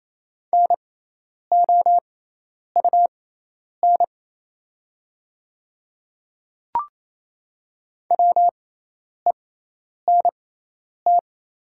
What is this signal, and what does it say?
Morse code, heard as a keyed tone, about 28 wpm, spaced as for 10 wpm, 700 Hz. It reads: NOUN WENT
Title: 6x Character Spacing - Top 500 Words - 28wpm